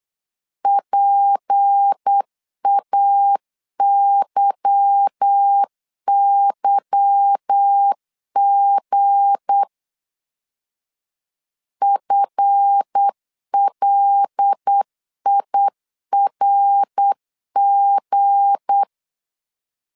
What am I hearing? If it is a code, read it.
Morse code, heard as a keyed tone, about 8 wpm, 800 Hz. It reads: PAYYG FLIRG
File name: LEZIONI / 8 YJ45 / 8 Y 8wpm 3245